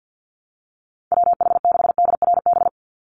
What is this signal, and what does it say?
Morse code, heard as a keyed tone, about 50 wpm, 700 Hz. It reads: WH6DRB
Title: USA Call Signs 50wpm